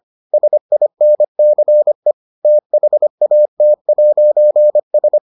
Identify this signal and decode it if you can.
Morse code, heard as a keyed tone, about 25 wpm, 600 Hz. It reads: SINCE THAT'S